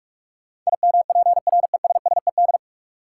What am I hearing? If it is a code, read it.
Morse code, heard as a keyed tone, about 43 wpm, 700 Hz. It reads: IMPRESSED
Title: Sentences from Top 1600 Words - New Word Drill Alt - 45wpm